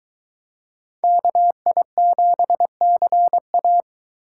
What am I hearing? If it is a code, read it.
Morse code, heard as a keyed tone, about 23 wpm, 700 Hz. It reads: KI7CA